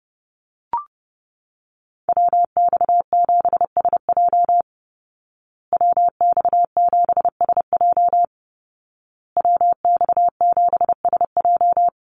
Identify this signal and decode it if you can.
Morse code, heard as a keyed tone, about 30 wpm, 700 Hz. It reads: WX7SJ WX7SJ WX7SJ
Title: Call Signs Repeated 2x - 30wpm